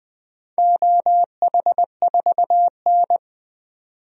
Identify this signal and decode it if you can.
Morse code, heard as a keyed tone, about 21 wpm, 700 Hz. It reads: OH4N
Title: DXCC-100 Call Signs - 4 Characters - 20wpm